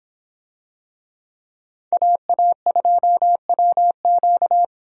Morse code, heard as a keyed tone, about 26 wpm, 700 Hz. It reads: AA2WQ